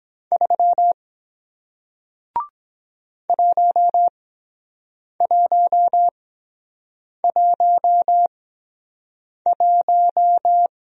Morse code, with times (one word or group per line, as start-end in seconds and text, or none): 0.3-0.9 3
3.3-10.7 1111